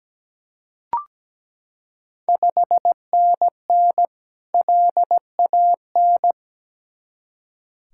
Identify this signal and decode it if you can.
Morse code, heard as a keyed tone, about 17 wpm, 700 Hz. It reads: 5NN LAN
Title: Ontario QSO Party Exchange 17wpm